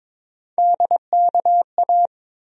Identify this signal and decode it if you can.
Morse code, heard as a keyed tone, about 22 wpm, 700 Hz. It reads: DKA
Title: USA Call Sign Postfixes - 3 Letters 22wpm